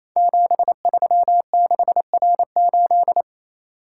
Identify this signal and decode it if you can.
Morse code, heard as a keyed tone, about 27 wpm, 700 Hz. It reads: 736R8
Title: Code Groups - 5 Letter:Numbers - 28wpm